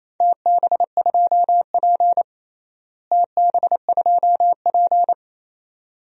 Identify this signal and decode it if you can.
Morse code, heard as a keyed tone, about 28 wpm, 700 Hz. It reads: TB2P TB2P
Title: DXCC-100 Call Signs - 4 Characters - Repeated 2x - 28wpm